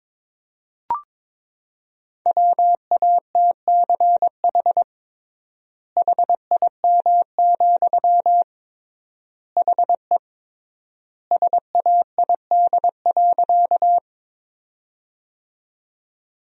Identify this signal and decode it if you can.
Morse code, heard as a keyed tone, about 22 wpm, 700 Hz. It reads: WATCH HIM, HE SAID.
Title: 3x Word-Spacing - Sentences from Top 300 Words - 22wpm